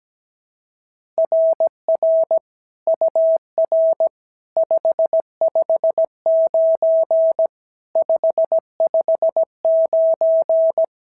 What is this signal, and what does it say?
Morse code, heard as a keyed tone, about 17 wpm, 650 Hz. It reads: RR UR 559 559
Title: SOTA QSO Practice 17wpm